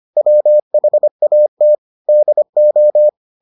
Morse code, heard as a keyed tone, about 25 wpm, 600 Hz. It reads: WHAT DO